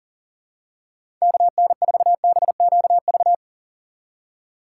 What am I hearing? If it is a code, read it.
Morse code, heard as a keyed tone, about 40 wpm, 700 Hz. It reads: KN4BQV
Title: USA Call Signs - 6 Characters - 6x Character Spacing - 40wpm